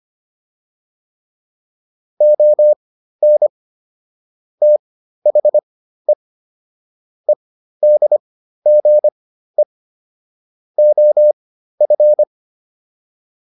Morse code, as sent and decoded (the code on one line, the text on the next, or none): --- -. / - .... / . -.. --. / --- ..-.
ON THE EDGE OF